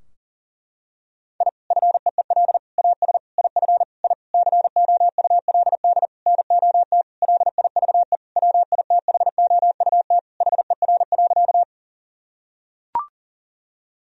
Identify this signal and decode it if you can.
Morse code, heard as a keyed tone, about 40 wpm, 700 Hz. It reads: I FEEL AS IF I COULD NOT LIVE WITHOUT HER.
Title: Sentences from Top 1000 Words - Review Part 3 - 40wpm